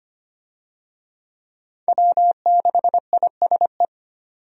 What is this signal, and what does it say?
Morse code, heard as a keyed tone, about 25 wpm, 690 Hz. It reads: W6ISE
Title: USA Call Signs - 5 Characters - 25wpm